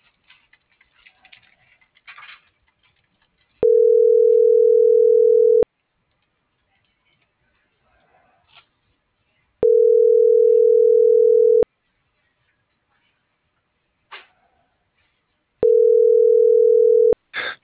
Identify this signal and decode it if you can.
Ringback tone (North America), 440 + 480 Hz.